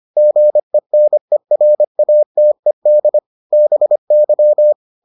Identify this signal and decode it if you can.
Morse code, heard as a keyed tone, about 25 wpm, 600 Hz. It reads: GENERATED BY